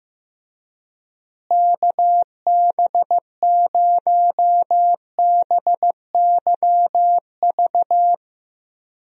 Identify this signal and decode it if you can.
Morse code, heard as a keyed tone, about 15 wpm, 700 Hz. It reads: KB0BYV
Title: USA Call Signs - 6 Characters - Repeated 3x - 15wpm